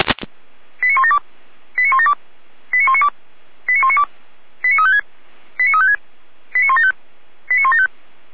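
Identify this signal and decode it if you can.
Five-tone selective call (ZVEI), heard as quick runs of five tones, tones of about 70 ms. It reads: 89172 89172 89192 89192 89356 89356 89175 89175
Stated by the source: ZVEI five-tone call received by SDR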